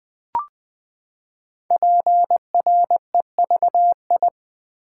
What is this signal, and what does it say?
Morse code, heard as a keyed tone, about 20 wpm, 700 Hz. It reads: PREVI
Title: Build A Word - 5 Letters At A Time - 10-Letter Words - 20wpm